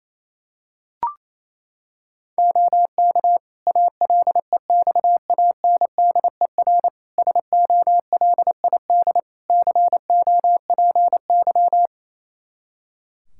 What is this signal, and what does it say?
Morse code, heard as a keyed tone, about 28 wpm, 700 Hz. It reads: OK ALEXANDER SOLID COPY
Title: K1USN SST QSO Practice 28wpm